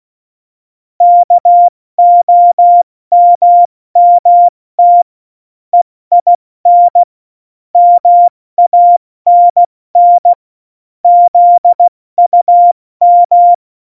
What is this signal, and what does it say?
Morse code, heard as a keyed tone, about 16 wpm, 700 Hz. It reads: KOMMT EIN MANN ZUM